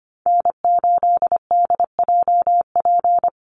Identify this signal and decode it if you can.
Morse code, heard as a keyed tone, about 24 wpm, 700 Hz. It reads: N8DJP
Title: USA Call Signs 25wpm